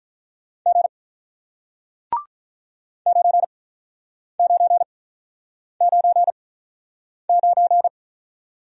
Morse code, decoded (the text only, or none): G 9999